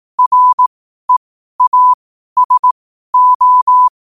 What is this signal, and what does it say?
Morse code, heard as a keyed tone, about 18 wpm, spaced as for 12 wpm, 1000 Hz. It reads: REASO
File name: qotdfiles / 12wpm